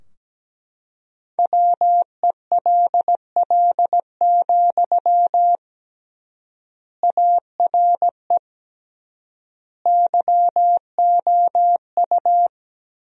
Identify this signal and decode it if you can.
Morse code, heard as a keyed tone, about 17 wpm, 700 Hz. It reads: WELL, ARE YOU
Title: 3x Word-Spacing - Sentences from Top 400 Words - 17wpm